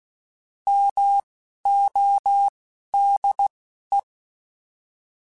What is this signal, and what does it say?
Morse code, heard as a keyed tone, about 16 wpm, spaced as for 11 wpm, 800 Hz. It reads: MODE
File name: Alphabetical / Alphabetical mp3s / Ms @10